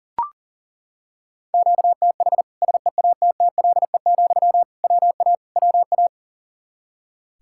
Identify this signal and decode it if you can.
Morse code, heard as a keyed tone, about 40 wpm, 700 Hz. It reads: QTH SEATTLE, WA WA
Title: Practice QSO - Exchange 1 - Beginning - 40wpm